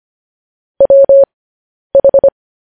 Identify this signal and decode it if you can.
Morse code, heard as a keyed tone, about 25 wpm, spaced as for 10 wpm, 550 Hz. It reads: WH
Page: news-headlines-in-morse-code-at-10-wpm